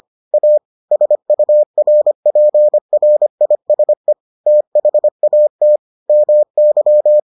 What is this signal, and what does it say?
Morse code, heard as a keyed tone, about 25 wpm, 600 Hz. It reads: A SURPRISE THAT MY